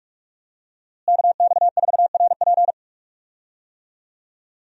Morse code, heard as a keyed tone, about 46 wpm, 700 Hz. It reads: KX4RP
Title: USA Call Signs - 5 Characters - Repeated 2x - 45wpm